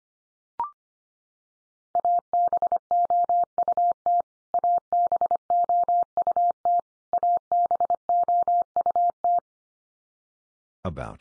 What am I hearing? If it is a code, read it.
Morse code, heard as a keyed tone, about 25 wpm, 700 Hz. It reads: ABOUT ABOUT ABOUT
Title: Top 100 Words 25wpm x3